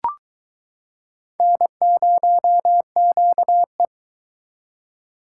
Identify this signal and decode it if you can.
Morse code, heard as a keyed tone, about 22 wpm, 700 Hz. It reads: N0QE